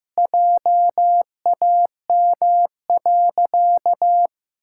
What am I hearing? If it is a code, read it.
Morse code, heard as a keyed tone, about 15 wpm, 700 Hz. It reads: JAM.
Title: Pangram Challenge v2 - 15wpm